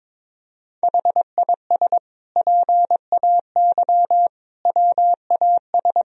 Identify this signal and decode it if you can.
Morse code, heard as a keyed tone, about 22 wpm, 700 Hz. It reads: HIS PAY WAS
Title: Sentences from Top 1000 Words - Review Part 1 - 22wpm